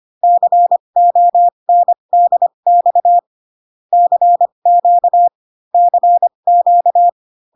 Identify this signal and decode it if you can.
Morse code, heard as a keyed tone, about 25 wpm, spaced as for 22 wpm, 700 Hz. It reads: CONDX CQ CQ
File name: CW common abbreviations alphabetical repeated 3 times 25 18wpm